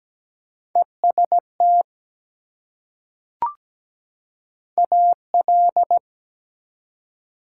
Morse code, heard as a keyed tone, about 17 wpm, 700 Hz. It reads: EST AL
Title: Prefixes and Suffixes 17wpm